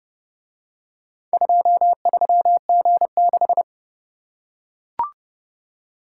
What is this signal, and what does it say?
Morse code, heard as a keyed tone, about 30 wpm, 700 Hz. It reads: 23G6